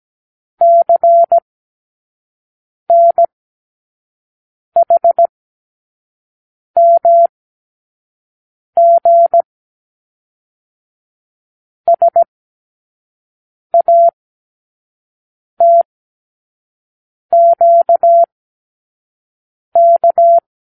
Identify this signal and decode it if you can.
Morse code, heard as a keyed tone, about 17 wpm, spaced as for 5 wpm, 680 Hz. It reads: CNHMG SATQK
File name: k7qo / Lesson 36 (A-X)